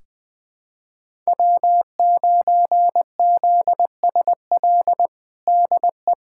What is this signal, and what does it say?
Morse code, heard as a keyed tone, about 21 wpm, 700 Hz. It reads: W9ZSL DE
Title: USA Callsign DE Callsign - 5 Characters - 20wpm